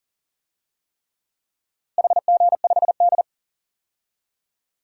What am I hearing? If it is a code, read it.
Morse code, heard as a keyed tone, about 40 wpm, 700 Hz. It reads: HG5D
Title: DXCC-100 Call Signs - 4 Characters - Repeated 3x - 40wpm